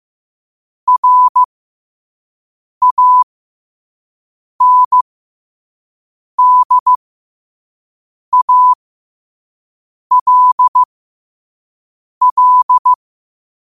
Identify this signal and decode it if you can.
Morse code, heard as a keyed tone, about 15 wpm, spaced as for 5 wpm, 1000 Hz. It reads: RANDALL
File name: qotdfiles / 5wpm